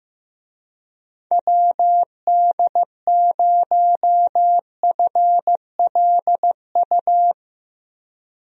Morse code, heard as a keyed tone, about 15 wpm, 700 Hz. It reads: WD0FLU
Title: USA Call Signs - 6 Characters - Repeated 3x - 15wpm